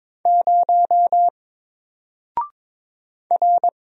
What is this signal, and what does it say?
Morse code, heard as a keyed tone, about 22 wpm, 700 Hz. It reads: 0 R
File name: Single-Letter-Number-22wpm